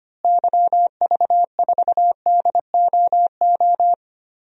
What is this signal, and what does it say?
Morse code, heard as a keyed tone, about 25 wpm, 700 Hz. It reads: YV4DOO
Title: DXCC-100 Call Signs - 6 Characters - Repeated 3x - 25wpm